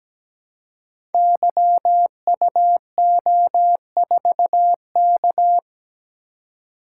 Morse code, heard as a keyed tone, about 17 wpm, 700 Hz. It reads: YUO4K
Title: Code Groups - 5 Letter:Numbers - 17wpm